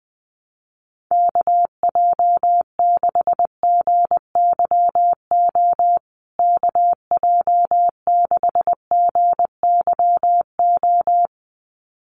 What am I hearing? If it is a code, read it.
Morse code, heard as a keyed tone, about 20 wpm, 700 Hz. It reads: KJ6GYO KJ6GYO
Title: USA Call Signs Repeated 2x 20wpm